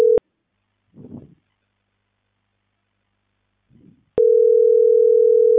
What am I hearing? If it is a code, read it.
Ringback tone (North America), 440 + 480 Hz.